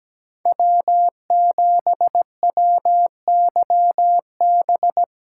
Morse code, heard as a keyed tone, about 17 wpm, 700 Hz. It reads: W7WYB
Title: USA Call Signs - 5 Characters - 6x Character Spacing - 17wpm